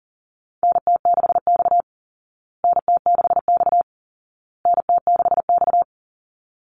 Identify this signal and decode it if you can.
Morse code, heard as a keyed tone, about 40 wpm, 700 Hz. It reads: NT6X NT6X NT6X
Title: Call Signs Repeated 2x - 40wpm